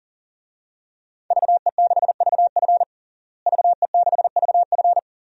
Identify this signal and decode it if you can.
Morse code, heard as a keyed tone, about 40 wpm, 700 Hz. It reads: VE6VF VE6VF
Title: DXCC-100 Call Signs - 5 Characters - Repeated 2x - 40wpm